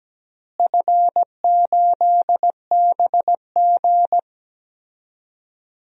Morse code, heard as a keyed tone, about 17 wpm, 700 Hz. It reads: F8BG
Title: DXCC-100 Call Signs - 4 Characters - Repeated 2x - 17wpm